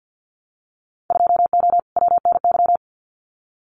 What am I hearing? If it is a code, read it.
Morse code, heard as a keyed tone, about 50 wpm, 700 Hz. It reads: 2O WNY
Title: Winter Field Day Exchanges Encore 50wpm